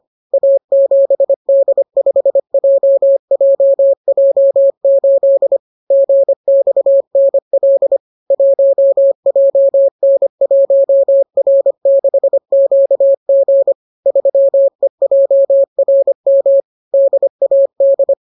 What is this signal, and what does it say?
Morse code, heard as a keyed tone, about 25 wpm, 550 Hz. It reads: A7D5JJJ8 GXNL 1JN1R6QG 3EJRM DAD